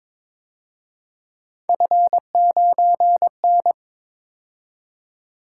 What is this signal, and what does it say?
Morse code, heard as a keyed tone, about 26 wpm, 700 Hz. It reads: F9N
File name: Speed-Racing - Three Letter-Number Combinations - 22wpm